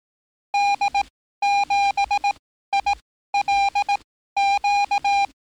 Morse code, heard as a keyed tone, about 18 wpm, spaced as for 13 wpm, 810 Hz. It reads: D7ILQ
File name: training / NAVPERS 11049to11060 / NAVPERS 11054 RA RB